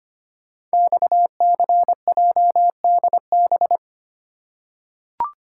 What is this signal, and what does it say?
Morse code, heard as a keyed tone, about 25 wpm, 700 Hz. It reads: XCJDB
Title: Code Groups - 5 Letters - 25wpm